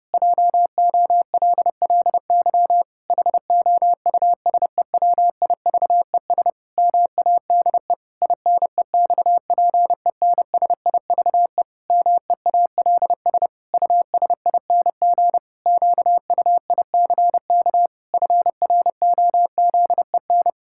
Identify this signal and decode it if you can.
Morse code, heard as a keyed tone, about 30 wpm, 700 Hz. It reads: JOLLY HOUSEWIVES MADE INEXPENSIVE MEALS USING QUICK FROZEN
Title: Pangram Challenge v2 - 30wpm